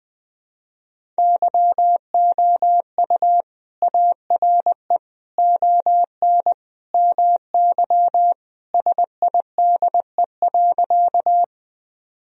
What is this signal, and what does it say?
Morse code, heard as a keyed tone, about 20 wpm, 700 Hz. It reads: YOU ARE ON MY SIDE.
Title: Sentences from Top 1000 Words - Review Part 1 - 20wpm